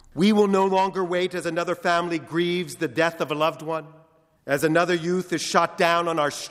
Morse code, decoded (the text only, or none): none